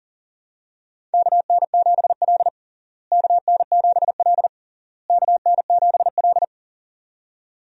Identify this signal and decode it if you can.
Morse code, heard as a keyed tone, about 37 wpm, 700 Hz. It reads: KN7L KN7L KN7L
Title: USA Call Signs - 4 Characters - Repeated 3x - 40wpm